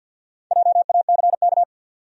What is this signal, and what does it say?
Morse code, heard as a keyed tone, about 49 wpm, 700 Hz. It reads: JACK
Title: USA Common Names - 50wpm